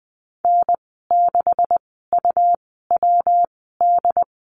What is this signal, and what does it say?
Morse code, heard as a keyed tone, about 20 wpm, spaced as for 15 wpm, 700 Hz. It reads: N6UWD
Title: 147 Lesson 24g - Letter y - Call Signs 20wpm